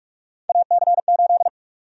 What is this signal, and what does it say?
Morse code, heard as a keyed tone, about 46 wpm, 700 Hz. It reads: AC8
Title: USA Call Sign Prefixes - 45wpm